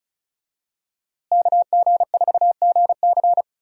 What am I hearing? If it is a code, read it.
Morse code, heard as a keyed tone, about 34 wpm, 700 Hz. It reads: KG4GC
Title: USA Call Signs - 5 Characters - Repeated 3x - 35wpm